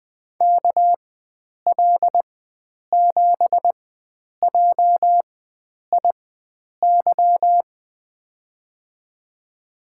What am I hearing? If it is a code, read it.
Morse code, heard as a keyed tone, about 20 wpm, spaced as for 9 wpm, 700 Hz. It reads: KL7JIY
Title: USA Call Signs - 6 Characters - 4x Character Spacing - 20wpm